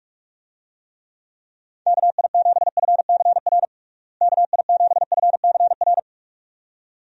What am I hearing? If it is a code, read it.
Morse code, heard as a keyed tone, about 41 wpm, 700 Hz. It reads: KI7FCR KI7FCR
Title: USA Call Signs - 6 Characters - Repeated 2x - 45wpm